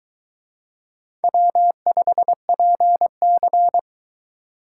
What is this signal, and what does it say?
Morse code, heard as a keyed tone, about 22 wpm, 700 Hz. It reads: W5PC